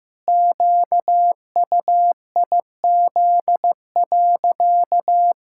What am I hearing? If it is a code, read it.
Morse code, heard as a keyed tone, about 15 wpm, 700 Hz. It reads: QUIZ.
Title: Pangram Challenge v2 - 15wpm